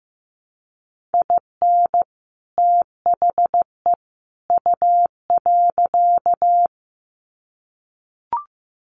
Morse code, heard as a keyed tone, about 15 wpm, 700 Hz. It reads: IN THE U.
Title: Sets of 3 Words 15wpm